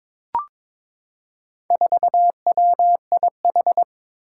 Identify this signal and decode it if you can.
Morse code, heard as a keyed tone, about 22 wpm, 700 Hz. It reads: 4WIH
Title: Four Challenging Letter-Number Combinations 22wpm